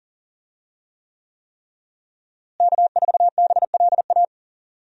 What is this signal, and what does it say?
Morse code, heard as a keyed tone, about 40 wpm, 700 Hz. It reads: K4BLA
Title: USA Call Signs - 5 Characters - Repeated 3x - 40wpm